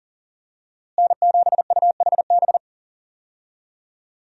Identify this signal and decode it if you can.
Morse code, heard as a keyed tone, about 38 wpm, 700 Hz. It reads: N7UHB
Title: USA Call Signs - 5 Characters - Repeated 3x - 40wpm